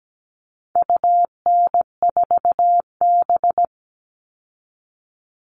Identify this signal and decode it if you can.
Morse code, heard as a keyed tone, about 17 wpm, 700 Hz. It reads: UN4B
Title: All Cut Numbers - 4-Digit 17wpm